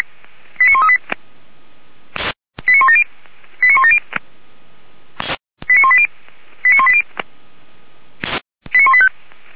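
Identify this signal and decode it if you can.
Five-tone selective call (ZVEI), heard as quick runs of five tones, tones of about 70 ms. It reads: 89138 89170 89170 89180 89180 89185
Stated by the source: ZVEI five-tone selective call received on SDR, from the Feuerwehr Orlamünde (fire brigade) audio server.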